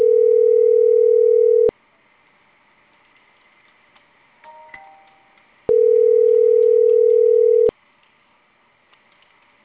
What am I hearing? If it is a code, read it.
Ringback tone (North America), 440 + 480 Hz.